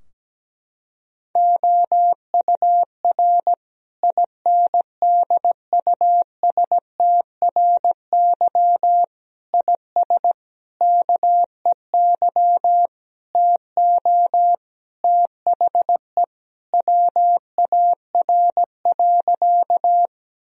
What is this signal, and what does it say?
Morse code, heard as a keyed tone, about 17 wpm, 700 Hz. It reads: OUR INDUSTRY IS KEY TO THE WAR.
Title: Sentences from Top 1000 Words - Review Part 4 - 17wpm